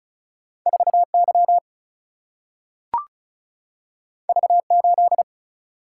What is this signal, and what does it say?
Morse code, heard as a keyed tone, about 36 wpm, 700 Hz. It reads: 4Y V8
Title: Single-Letter Single-Number 35wpm